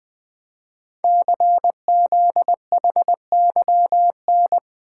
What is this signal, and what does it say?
Morse code, heard as a keyed tone, about 20 wpm, 700 Hz. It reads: CZHYN
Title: Code Groups - 5 Letters - 20wpm